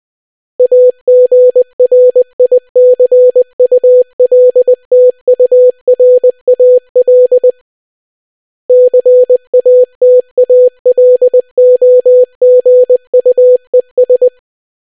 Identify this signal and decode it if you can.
Morse code, heard as a keyed tone, about 20 wpm, 500 Hz. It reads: AGRICULTURAL CATALOGUES